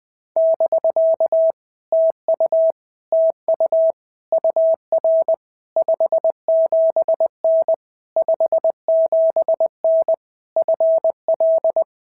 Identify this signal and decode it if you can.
Morse code, heard as a keyed tone, about 20 wpm, 650 Hz. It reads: <BK> TU TU UR 57N 57N FL
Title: POTA QSO Practice - 20wpm